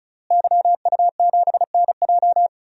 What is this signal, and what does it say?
Morse code, heard as a keyed tone, about 34 wpm, 700 Hz. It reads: YU7NJ
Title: DXCC-100 Call Signs - 5 Characters - Repeated 2x - 35wpm